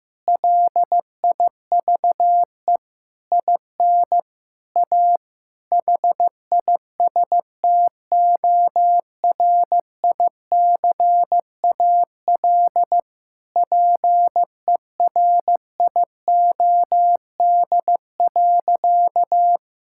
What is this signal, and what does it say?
Morse code, heard as a keyed tone, about 15 wpm, 700 Hz. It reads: LIVE IN A HISTORICAL PERIOD.
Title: Sentences from Top 1000 Words - Review Part 5 - 15wpm